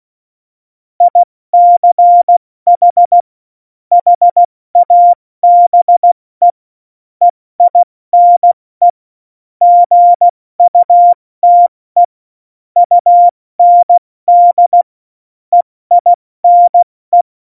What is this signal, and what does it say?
Morse code, heard as a keyed tone, about 16 wpm, 700 Hz. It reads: ICH HABE EINE GUTE UND EINE
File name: Morsekurs / Witze / Witz 22. September 2020, 70 F80